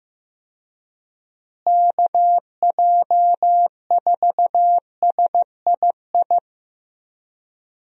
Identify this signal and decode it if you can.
Morse code, heard as a keyed tone, about 15 wpm, 700 Hz. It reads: KJ4SII
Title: USA Call Signs - 6 Characters - Repeated 2x - 15wpm